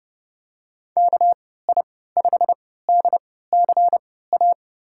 Morse code, heard as a keyed tone, about 31 wpm, spaced as for 17 wpm, 700 Hz. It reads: KI5DCA
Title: USA Call Signs - 6 Characters - 3x Character Spacing - 30wpm